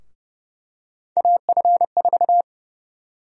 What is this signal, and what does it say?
Morse code, heard as a keyed tone, about 31 wpm, 700 Hz. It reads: AF4
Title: USA Call Sign Prefixes - 30wpm